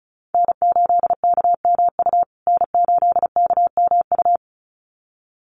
Morse code, heard as a keyed tone, about 34 wpm, 700 Hz. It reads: N8KMU N8KMU
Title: USA Call Signs Repeated 2x 35wpm